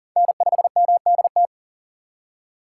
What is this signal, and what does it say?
Morse code, heard as a keyed tone, about 40 wpm, 700 Hz. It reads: N5MDT
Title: USA Call Signs - 6x Character Spacing - 40wpm